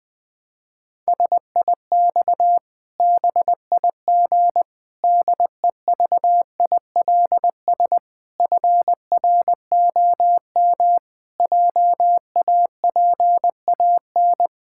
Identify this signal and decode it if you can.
Morse code, heard as a keyed tone, about 21 wpm, 700 Hz. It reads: SIX BIG DEVILS FROM JAPAN
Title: Pangram Challenge v2 - 20wpm